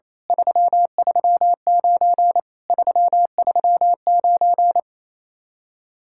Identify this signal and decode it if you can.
Morse code, heard as a keyed tone, about 26 wpm, 700 Hz. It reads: 339 339
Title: Speed-Racing Common QSO Phrases - 28wpm